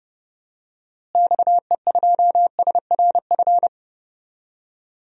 Morse code, heard as a keyed tone, about 30 wpm, 700 Hz. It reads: XE2SRF
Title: DXCC-100 Call Signs - 6 Characters - Repeated 3x - 30wpm